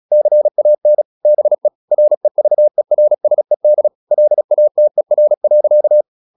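Morse code, heard as a keyed tone, about 36 wpm, 600 Hz. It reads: CAN BE REVERSED LATER.
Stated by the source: Morse practice at 36 wpm, text of Ken Follett's novel Never, from CW med Gnister.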